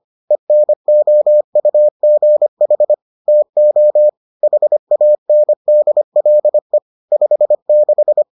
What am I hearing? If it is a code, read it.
Morse code, heard as a keyed tone, about 25 wpm, 600 Hz. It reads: ENOUGH TO HANDLE 56